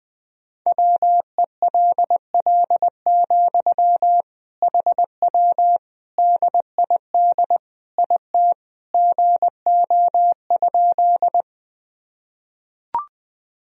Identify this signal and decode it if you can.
Morse code, heard as a keyed tone, about 19 wpm, 700 Hz. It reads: WELL, HW DID IT GO?
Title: Sentences from Top 200 Words and Abbreviations 20wpm